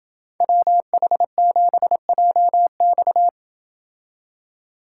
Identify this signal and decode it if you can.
Morse code, heard as a keyed tone, about 27 wpm, 700 Hz. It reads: WH7JX